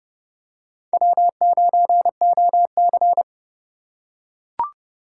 Morse code, heard as a keyed tone, about 32 wpm, 700 Hz. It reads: W9OC
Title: USA Call Signs - 4x Character Spacing - 30wpm